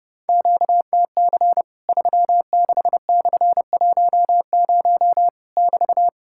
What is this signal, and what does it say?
Morse code, heard as a keyed tone, about 28 wpm, 700 Hz. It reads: QTC 36/10 <BT>
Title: Worked All Europe DX Contest - QTC Exchange - 30wpm